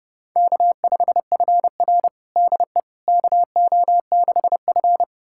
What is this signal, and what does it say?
Morse code, heard as a keyed tone, about 30 wpm, 700 Hz. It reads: K5FR DE KO6F
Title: USA Callsign DE Callsign - 4 Characters - 30wpm